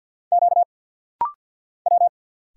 Morse code, heard as a keyed tone, about 50 wpm, 700 Hz. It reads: Q W
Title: Single Letters - Rapid-Fire 50wpm